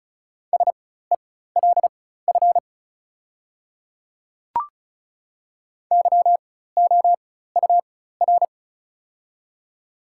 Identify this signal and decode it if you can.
Morse code, heard as a keyed tone, about 36 wpm, spaced as for 16 wpm, 700 Hz. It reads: SELF YOUR
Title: 4x Character Spacing - Top 200 Words - 35wpm